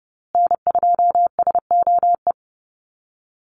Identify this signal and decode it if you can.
Morse code, heard as a keyed tone, about 30 wpm, 700 Hz. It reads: N2SOE
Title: USA Call Signs Repeated 2x 30wpm